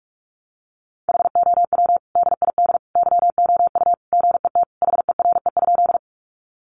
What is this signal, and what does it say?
Morse code, heard as a keyed tone, about 44 wpm, 700 Hz. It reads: HOW DID YOU GET HERE?
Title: Sentences from Top 200 Words 45wpm